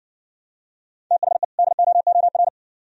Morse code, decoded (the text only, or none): THE DOOR